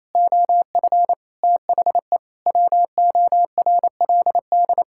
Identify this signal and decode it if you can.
Morse code, heard as a keyed tone, about 28 wpm, 700 Hz. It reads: OF THE WORLD
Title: Sentences from Top 1600 Words 28wpm